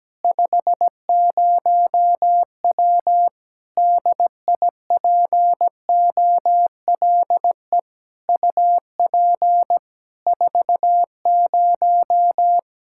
Morse code, heard as a keyed tone, about 17 wpm, 700 Hz. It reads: 50W DIPOLE UP 40
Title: K1USN SST QSO Practice 17wpm